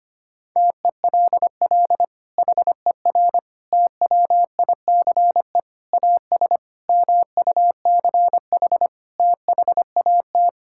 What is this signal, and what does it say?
Morse code, heard as a keyed tone, about 25 wpm, 700 Hz. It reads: TELL HER TWICE AS MUCH THAT
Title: Sentences from Top 1600 Words 25wpm